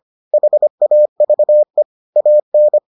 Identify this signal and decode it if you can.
Morse code, heard as a keyed tone, about 25 wpm, 600 Hz. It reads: HAVE AN